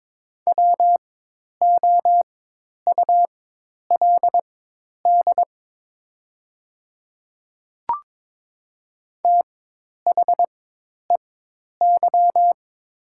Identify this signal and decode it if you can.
Morse code, heard as a keyed tone, about 22 wpm, spaced as for 10 wpm, 700 Hz. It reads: WOULD THEY
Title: 4x Character Spacing - Top 200 Words - 22wpm